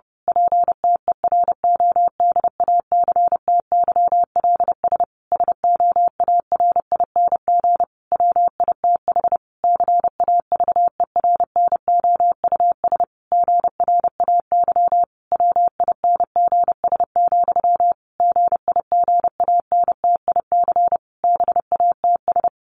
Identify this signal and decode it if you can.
Morse code, heard as a keyed tone, about 30 wpm, 700 Hz. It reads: PTERODACTYLS SOARING WITH CAVERNOUS GRAY WINGS, GIGANTIC BATS
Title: A Sound Of Thunder 30wpm